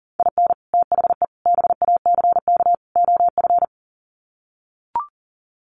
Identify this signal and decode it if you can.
Morse code, heard as a keyed tone, about 37 wpm, 700 Hz. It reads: IN THE BACK OF
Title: Sets of 4 Words 40wpm